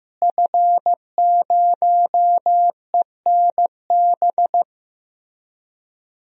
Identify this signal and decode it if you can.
Morse code, heard as a keyed tone, about 15 wpm, 700 Hz. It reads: F0ENB
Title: DXCC-100 Call Signs - 5 Characters - Repeated 3x - 15wpm